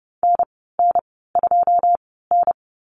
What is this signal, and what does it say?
Morse code, heard as a keyed tone, about 30 wpm, spaced as for 17 wpm, 700 Hz. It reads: NN2N
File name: Lesson 14g - Number 2 - Call Signs 30wpm